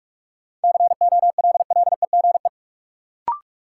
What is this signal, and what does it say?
Morse code, heard as a keyed tone, about 41 wpm, 700 Hz. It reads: COLLEGE
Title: Sentences from Top 1300 Words - New Word Drill Alt - 45wpm